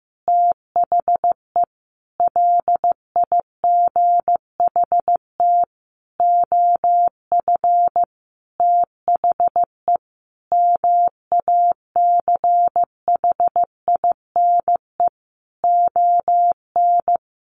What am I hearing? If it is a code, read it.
Morse code, heard as a keyed tone, about 15 wpm, 700 Hz. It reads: THE LIGHT OF THE MACHINE ON
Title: A Sound Of Thunder 15wpm